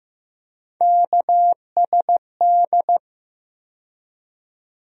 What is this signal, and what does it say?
Morse code, heard as a keyed tone, about 15 wpm, 700 Hz. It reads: KSD
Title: USA Call Sign Postfixes - 3 Letters 15wpm